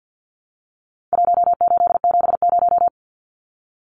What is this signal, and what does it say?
Morse code, heard as a keyed tone, about 48 wpm, 700 Hz. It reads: 1870
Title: Numbers - Contest Style - 4-Digit 50wpm